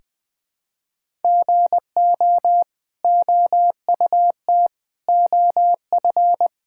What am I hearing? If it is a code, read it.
Morse code, heard as a keyed tone, about 20 wpm, 700 Hz. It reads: GO OUT OF
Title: Sets of 6 Words - 20wpm